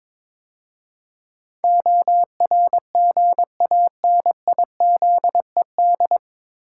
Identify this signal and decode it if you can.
Morse code, heard as a keyed tone, about 22 wpm, 700 Hz. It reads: ORGANIZED
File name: 9 Letter Words - First 3 Letters Given - 22wpm